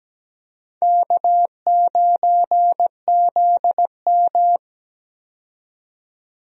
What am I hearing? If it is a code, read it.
Morse code, heard as a keyed tone, about 17 wpm, 700 Hz. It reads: K9ZM
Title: USA Call Signs - 4 Characters - 17wpm